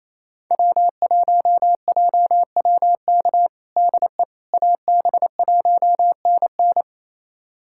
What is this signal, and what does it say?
Morse code, heard as a keyed tone, about 28 wpm, 700 Hz. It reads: W1JWK DE AB1NN
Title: USA Callsign DE Callsign - 5 Characters - 28wpm